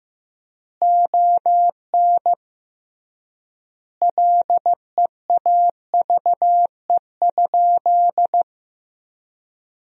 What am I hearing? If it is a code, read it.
Morse code, heard as a keyed tone, about 15 wpm, 700 Hz. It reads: ON LEAVE?
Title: 3x Word-Spacing - Sentences from Top 400 Words - 15wpm